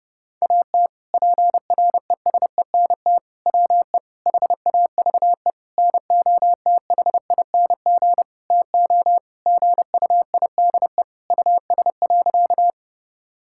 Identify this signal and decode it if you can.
Morse code, heard as a keyed tone, about 30 wpm, 700 Hz. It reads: AT PRESENT WE HAVE NOTHING TO GUIDE US.
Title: Sentences from Top 1000 Words - Review Part 5 - 30wpm